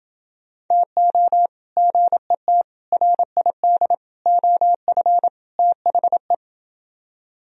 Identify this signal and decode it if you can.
Morse code, heard as a keyed tone, about 29 wpm, 700 Hz. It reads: TO GET RID OF THE